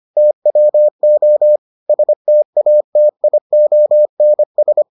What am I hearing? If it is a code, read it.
Morse code, heard as a keyed tone, about 25 wpm, 600 Hz. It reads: TWO STATIONS